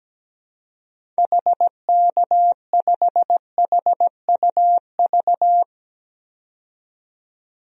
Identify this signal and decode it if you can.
Morse code, heard as a keyed tone, about 17 wpm, 700 Hz. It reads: HK5HUV
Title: DXCC-100 Call Signs - 6 Characters - 17wpm